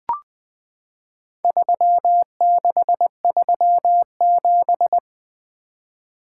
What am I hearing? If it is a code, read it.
Morse code, heard as a keyed tone, about 20 wpm, 700 Hz. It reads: 3637
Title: Four Letter-Number Combinations - 20wpm